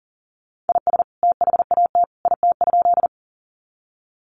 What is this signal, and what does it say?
Morse code, heard as a keyed tone, about 40 wpm, 700 Hz. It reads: IS THAT IT?
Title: Sentences from Top 100 Words 40wpm